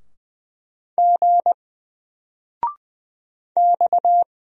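Morse code, heard as a keyed tone, about 20 wpm, 700 Hz. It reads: G X
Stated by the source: Single letter and number Morse drill at 20 wpm, from N4LCD.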